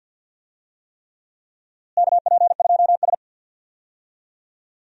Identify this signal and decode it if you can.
Morse code, heard as a keyed tone, about 53 wpm, 700 Hz. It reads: KP2S
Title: DXCC-100 Call Signs - 4 Characters - Repeated 3x - 50wpm